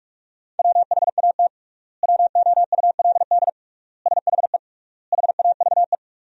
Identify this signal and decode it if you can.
Morse code, heard as a keyed tone, about 44 wpm, 690 Hz. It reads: WHAT WOULD SHE HAVE
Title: 3x Word-Spacing - Sentences from Top 100 Words - 45wpm